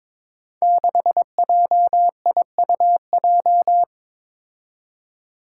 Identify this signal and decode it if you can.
Morse code, heard as a keyed tone, about 22 wpm, 700 Hz. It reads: 6JIUJ
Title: Five Challenging Letter-Number Combinations 22wpm